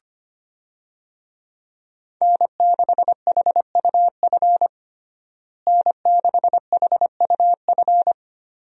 Morse code, heard as a keyed tone, about 25 wpm, 700 Hz. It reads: N6HUF N6HUF
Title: USA Call Signs - 5 Characters - Repeated 2x - 25wpm